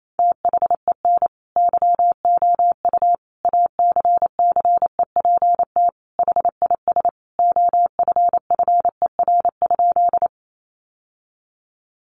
Morse code, heard as a keyed tone, about 28 wpm, 700 Hz. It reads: THEN YOU ACCEPT HIS OFFER?
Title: Sentences from Top 1300 Words 28wpm